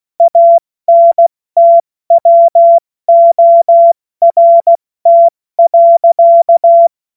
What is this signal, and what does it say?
Morse code, heard as a keyed tone, about 16 wpm, 680 Hz. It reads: ANTWORT.
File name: Morsekurs / Witze / Witz 03. Mai 2022 70 F80